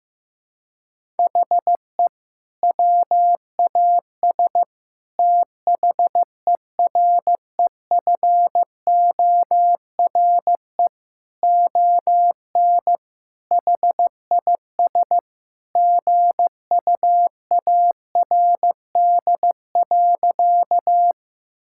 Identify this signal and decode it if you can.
Morse code, heard as a keyed tone, about 15 wpm, 700 Hz. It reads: HE WAS THEREFORE ON HIS GUARD.
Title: Sentences from Top 1000 Words - Review Part 5 - 15wpm